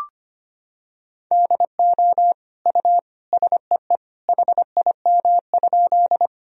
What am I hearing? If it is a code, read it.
Morse code, heard as a keyed tone, about 25 wpm, 700 Hz. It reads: DO U SEE HIM?
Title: Sentences from Top 100 Words and Abbreviations 25wpm